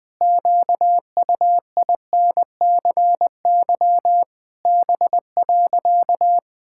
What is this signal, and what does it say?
Morse code, heard as a keyed tone, about 20 wpm, 700 Hz. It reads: QUINCY B.
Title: Pangram Challenge v2 - 20wpm